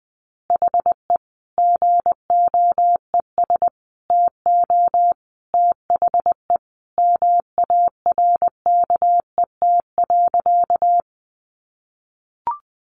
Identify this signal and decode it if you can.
Morse code, heard as a keyed tone, about 20 wpm, 700 Hz. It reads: HE GOES TO THE MARKET.
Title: Additional Easy Phrases 20wpm